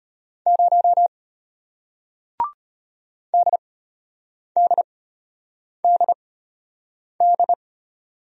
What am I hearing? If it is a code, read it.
Morse code, heard as a keyed tone, about 34 wpm, spaced as for 8 wpm, 700 Hz. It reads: 0 DDDD